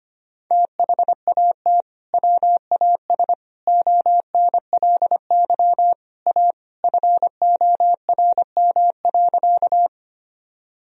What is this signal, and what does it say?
Morse code, heard as a keyed tone, about 25 wpm, 700 Hz. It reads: THAT WAS ONLY A FORM.
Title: Sentences from Top 1000 Words - Review Part 1 - 25wpm